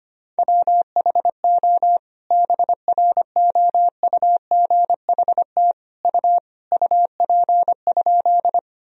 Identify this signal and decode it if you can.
Morse code, heard as a keyed tone, about 25 wpm, 700 Hz. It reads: WHO BROUGHT U UP?
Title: Sentences from Top 500 Words and Abbreviations 25wpm